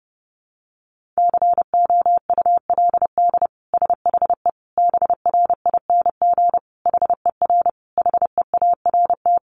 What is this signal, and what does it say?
Morse code, heard as a keyed tone, about 30 wpm, 700 Hz. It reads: COULD SHE BRING HER HEART
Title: Sentences from Top 1400 Words 30wpm